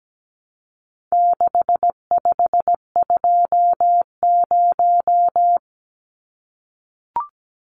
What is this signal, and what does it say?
Morse code, heard as a keyed tone, about 17 wpm, 700 Hz. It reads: 6520
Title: Numbers - 4-Digit 17wpm